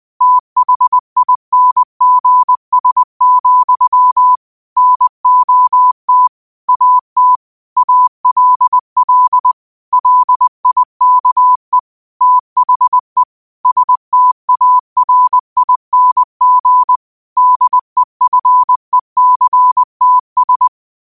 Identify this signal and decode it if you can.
Morse code, heard as a keyed tone, about 20 wpm, 1000 Hz. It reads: THINGS, NOT AT ALL LIKE THE STARING DEFECTS
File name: qotdfiles / 20wpm